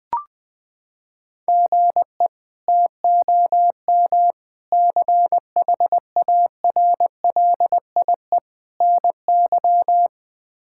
Morse code, heard as a keyed tone, about 20 wpm, 700 Hz. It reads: GE TOM CHARLIE NY
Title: K1USN SST QSO Practice - 20wpm